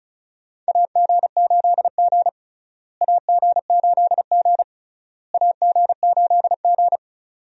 Morse code, heard as a keyed tone, about 34 wpm, 700 Hz. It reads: AG8G AG8G AG8G
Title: USA Call Signs - 4 Characters - Repeated 3x - 35wpm